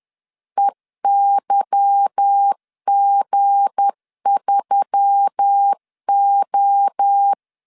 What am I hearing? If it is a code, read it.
Morse code, heard as a keyed tone, about 10 wpm, 800 Hz. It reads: EYG3O
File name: CW / DET / 1 9 1500C S B 10wpm 4439